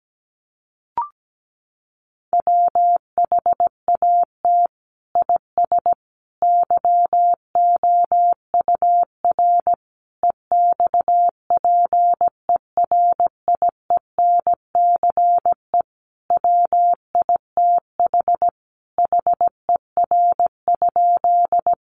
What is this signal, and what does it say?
Morse code, heard as a keyed tone, about 17 wpm, 700 Hz. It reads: WHAT IS YOUR EXPERIENCE WITH HER?
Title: Sentences from Top 600 Words 17wpm